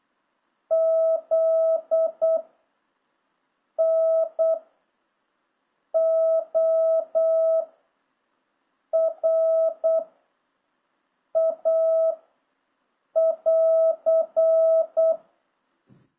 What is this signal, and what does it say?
Morse code, heard as a keyed tone, about 8 wpm, 660 Hz. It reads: Z N O R A <AR>